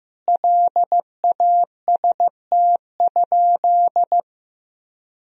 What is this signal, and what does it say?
Morse code, heard as a keyed tone, about 15 wpm, 700 Hz. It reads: LAST?
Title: Sentences from Top 1000 Words - Review Part 2 - 15wpm